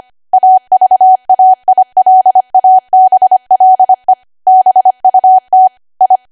Morse code, heard as a keyed tone, about 25 wpm, 750 Hz. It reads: AVAILABLE BUT I
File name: Archive / 25 WPM / 25WPM